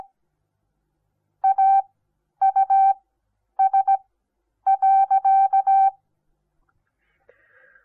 Morse code, heard as a keyed tone, about 17 wpm, spaced as for 9 wpm, 780 Hz. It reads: AUS.
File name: Onlinelehrgang / morsen / DJ4UF Morsen08